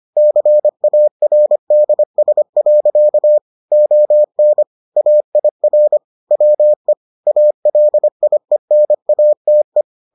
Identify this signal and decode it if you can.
Morse code, heard as a keyed tone, about 25 wpm, 600 Hz. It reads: CARDS. ON AIR WE ALIENATE